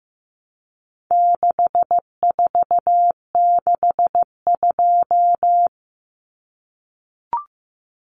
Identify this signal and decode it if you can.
Morse code, heard as a keyed tone, about 15 wpm, 700 Hz. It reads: 6462
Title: Numbers - 4-Digit 15wpm